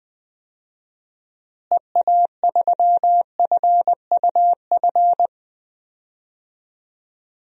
Morse code, heard as a keyed tone, about 20 wpm, 700 Hz. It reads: EA3FUF
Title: DXCC-100 Call Signs - 6 Characters - Repeated 3x - 20wpm